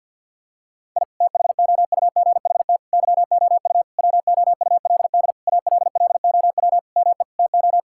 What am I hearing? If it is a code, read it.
Morse code, heard as a keyed tone, about 50 wpm, 700 Hz. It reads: I THOUGHT YOU WOULD ALLOW ME TO
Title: Sentences from Top 1000 Words - Review Part 5 - 50wpm